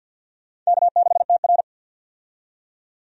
Morse code, heard as a keyed tone, about 50 wpm, 700 Hz. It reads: K6TR
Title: USA Call Signs - 4 Characters - 50wpm